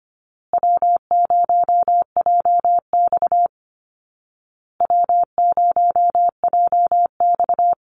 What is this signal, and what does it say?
Morse code, heard as a keyed tone, about 25 wpm, 700 Hz. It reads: W0JX W0JX
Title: Call Signs Repeated 2x - 25wpm